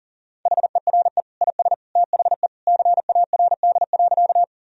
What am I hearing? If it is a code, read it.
Morse code, heard as a keyed tone, about 37 wpm, 700 Hz. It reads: HERE IS THE CARD.
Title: Sentences from Top 1000 Words - Review Part 4 - 40wpm